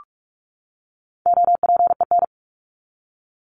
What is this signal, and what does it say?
Morse code, heard as a keyed tone, about 44 wpm, 700 Hz. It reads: OPEN
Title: ING Suffix - Top 100 Words with Base Words 45wpm